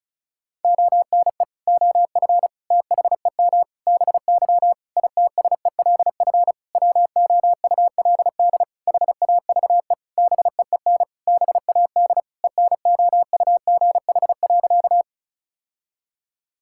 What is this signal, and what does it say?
Morse code, heard as a keyed tone, about 36 wpm, 700 Hz. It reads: ONE OF THEM BY ITSELF WOULD HAVE BEEN BAD ENOUGH.
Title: Sentences from Top 1000 Words - Review Part 5 - 35wpm